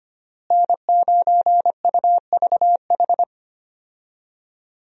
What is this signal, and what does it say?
Morse code, heard as a keyed tone, about 25 wpm, 700 Hz. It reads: N9UVH
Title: USA Call Signs - 5 Characters - Repeated 3x - 25wpm